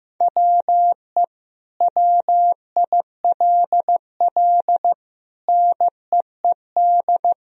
Morse code, heard as a keyed tone, about 15 wpm, 700 Hz. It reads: WE WILL NEED
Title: Sentences from Top 1000 Words - Review Part 4 - 15wpm